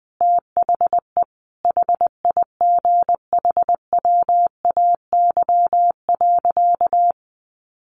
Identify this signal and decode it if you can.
Morse code, heard as a keyed tone, about 20 wpm, 700 Hz. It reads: THE HIGHWAY.
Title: Sentences from Top 1400 Words 20wpm